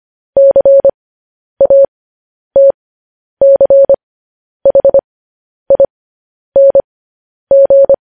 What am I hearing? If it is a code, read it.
Morse code, heard as a keyed tone, about 25 wpm, spaced as for 10 wpm, 550 Hz. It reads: CATCHING